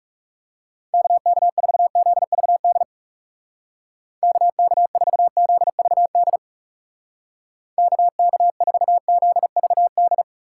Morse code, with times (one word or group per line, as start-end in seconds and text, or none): 0.9-2.8 KK4ZVD
4.2-6.4 KK4ZVD
7.8-10.2 KK4ZVD